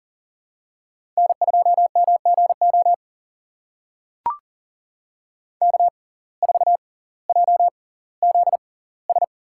Morse code, heard as a keyed tone, about 40 wpm, spaced as for 20 wpm, 700 Hz. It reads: N2MGO K4JZS
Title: USA Call Signs - 5 Characters - 6x Character Spacing - 40wpm